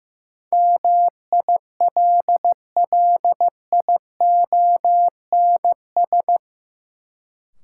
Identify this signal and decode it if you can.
Morse code, heard as a keyed tone, about 15 wpm, 700 Hz. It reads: MILLIONS
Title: Sets of 5 Words - New Word Drill - 15wpm